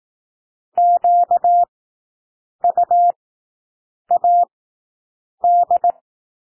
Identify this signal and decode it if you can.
Morse code, heard as a keyed tone, about 18 wpm, spaced as for 7 wpm, 700 Hz. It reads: QUAD
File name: SOUND WORDS 18 wpm